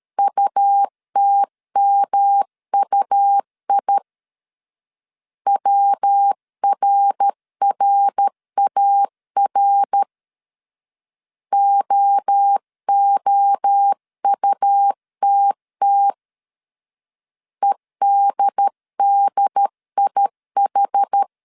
Morse code, heard as a keyed tone, about 13 wpm, 800 Hz. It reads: UTMUI WRRAR OOUTT EDDIH
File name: CW / DET / 1 3 1500C S A 12wpm 2905